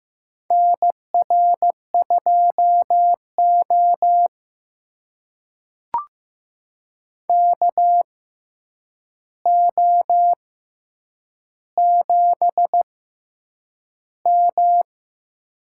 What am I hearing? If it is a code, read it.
Morse code, heard as a keyed tone, about 15 wpm, spaced as for 8 wpm, 700 Hz. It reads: NR2O KO7M
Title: USA Call Signs - 4 Characters - 6x Character Spacing - 15wpm